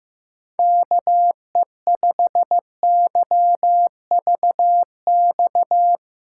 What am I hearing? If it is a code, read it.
Morse code, heard as a keyed tone, about 16 wpm, 700 Hz. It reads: KE5YVX
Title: USA Callsign DE Callsign - 6 Characters - 15wpm